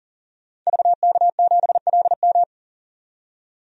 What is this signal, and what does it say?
Morse code, heard as a keyed tone, about 40 wpm, 700 Hz. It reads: VK7LM
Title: DXCC-100 Call Signs - 5 Characters - 40wpm